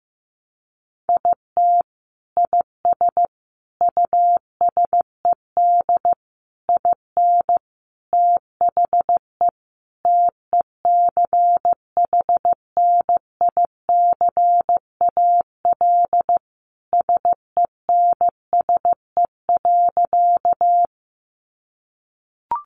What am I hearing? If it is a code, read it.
Morse code, heard as a keyed tone, about 15 wpm, 700 Hz. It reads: IT IS USED IN THE TECHNICAL SENSE.
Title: Sentences from Top 1100 Words 15wpm